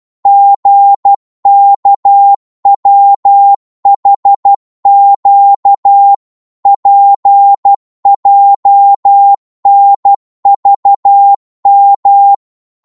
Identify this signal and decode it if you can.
Morse code, heard as a keyed tone, about 12 wpm, 800 Hz. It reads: GKWHQ PJNVM